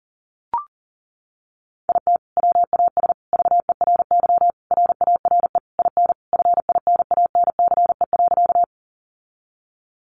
Morse code, heard as a keyed tone, about 40 wpm, 700 Hz. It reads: IT WAS VERY RARE IN FINANCE.
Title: Sentences from Top 1500 Words 40wpm